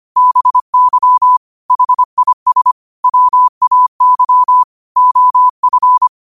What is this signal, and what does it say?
Morse code, heard as a keyed tone, about 25 wpm, 1000 Hz. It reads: BY HIS WAY OF
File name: qotdfiles / 25wpm